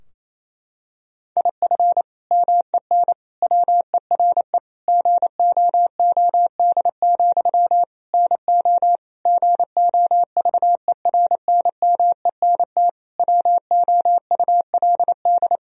Morse code, heard as a keyed tone, about 28 wpm, 700 Hz. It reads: IF MEN WERE GOOD, NO GOVERNMENT WOULD
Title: Sentences from Top 1000 Words - Review Part 3 - 28wpm